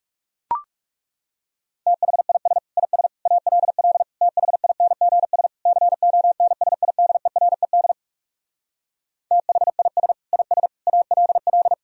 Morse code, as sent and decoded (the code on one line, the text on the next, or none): - .... .. ... / .. ... / .- .-.. .-.. / - .... .. -. --. ... / -.-. --- -. ... .. -.. .-. -.. / - .... .. ... / .. ... / .- .-.. .-..
THIS IS ALL THINGS CONSIDERED THIS IS ALL